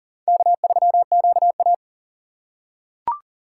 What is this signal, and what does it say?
Morse code, heard as a keyed tone, about 40 wpm, 690 Hz. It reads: K3QA
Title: USA Call Signs - 4 Characters - 2x Character Spacing - 40wpm